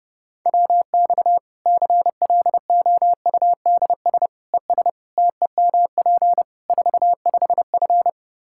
Morse code, heard as a keyed tone, about 30 wpm, 700 Hz. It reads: WX CLOUDS ES TEMP 45F
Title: Practice QSO - Exchange 2 - Info - 30wpm